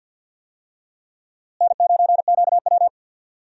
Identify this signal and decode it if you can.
Morse code, heard as a keyed tone, about 49 wpm, 700 Hz. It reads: N9QW
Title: USA Call Signs - 4 Characters - Repeated 2x - 50wpm